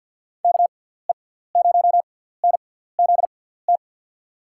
Morse code, heard as a keyed tone, about 49 wpm, spaced as for 17 wpm, 700 Hz. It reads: KE0NZT
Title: USA Call Signs - 6 Characters - 6x Character Spacing - 50wpm